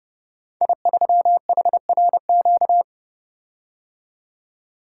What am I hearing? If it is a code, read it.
Morse code, heard as a keyed tone, about 28 wpm, 700 Hz. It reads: I3HRQ
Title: Code Groups - 5 Letter:Numbers - 30wpm